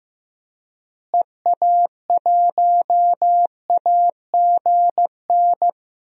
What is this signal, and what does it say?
Morse code, heard as a keyed tone, about 15 wpm, 700 Hz. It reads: EA1AGN
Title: DXCC-100 Call Signs - 6 Characters - Repeated 3x - 15wpm